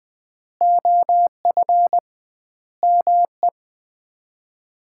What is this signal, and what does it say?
Morse code, heard as a keyed tone, about 20 wpm, 700 Hz. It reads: OF ME
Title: 2x Word-Spacing - Sets of 4 Words - 20wpm